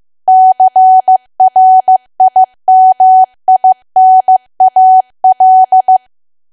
Morse code, heard as a keyed tone, about 15 wpm, 750 Hz. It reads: CRIMINAL
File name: cwa / W1AW-Level2 / 15WPM